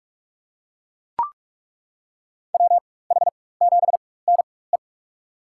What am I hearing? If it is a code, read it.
Morse code, heard as a keyed tone, about 42 wpm, spaced as for 20 wpm, 700 Hz. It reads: WH7NE
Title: USA Call Signs - 4x Character Spacing - 45wpm